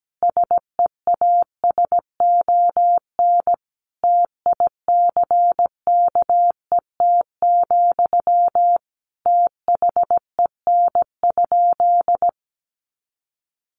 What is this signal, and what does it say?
Morse code, heard as a keyed tone, about 17 wpm, 700 Hz. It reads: SEASON TICKET, THEN?
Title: Sentences from Top 1400 Words 17wpm